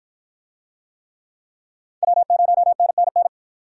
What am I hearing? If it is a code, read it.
Morse code, heard as a keyed tone, about 51 wpm, 700 Hz. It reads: W0NNN